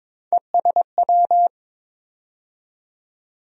Morse code, heard as a keyed tone, about 22 wpm, 700 Hz. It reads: ESW
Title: USA Call Sign Postfixes - 3 Letters 22wpm